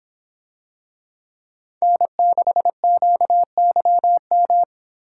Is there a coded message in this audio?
Yes: Morse code from 1.8 to 4.6 s, heard as a keyed tone, about 24 wpm, 700 Hz. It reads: N6QYM